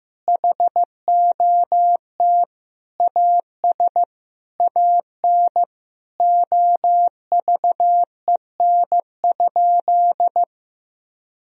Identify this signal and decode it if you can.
Morse code, heard as a keyed tone, about 15 wpm, 700 Hz. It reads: HOT AS AN OVEN?
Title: Sentences from Top 1000 Words - Review Part 4 - 15wpm